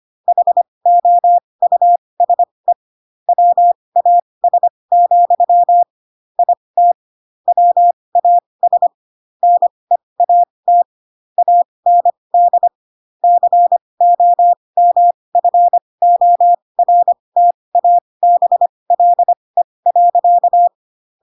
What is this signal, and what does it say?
Morse code, heard as a keyed tone, about 25 wpm, spaced as for 20 wpm, 700 Hz. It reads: HOUSE WAS, IT WAS NEAT AND COMFORTABLE.
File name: Pinocchio / Pinocchio ch3 15 25 ik1hge